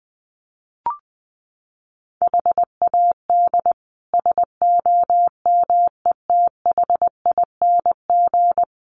Morse code, heard as a keyed tone, about 20 wpm, 700 Hz. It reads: HAD SOMETHING
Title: Sets of 5 Words 20wpm